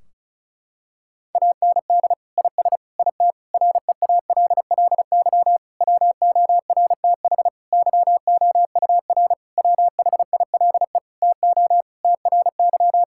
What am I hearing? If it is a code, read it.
Morse code, heard as a keyed tone, about 36 wpm, 700 Hz. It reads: AND IS IT REALLY WORTH YOUR WHILE TO TRY
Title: Sentences from Top 1600 Words 35wpm